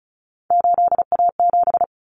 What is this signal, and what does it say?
Morse code, heard as a keyed tone, about 34 wpm, 700 Hz. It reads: 8A7
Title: Common Cut Numbers - 3-Digit 35wpm